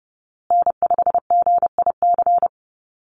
Morse code, heard as a keyed tone, about 30 wpm, 700 Hz. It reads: N5GIC
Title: USA Call Signs Repeated 2x 30wpm